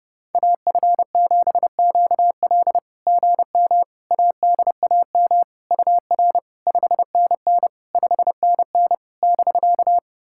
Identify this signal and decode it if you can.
Morse code, heard as a keyed tone, about 30 wpm, 690 Hz. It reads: AF7QL GM ADAM UR 5NN 5NN <BK>
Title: SOTA QSO Practice 30wpm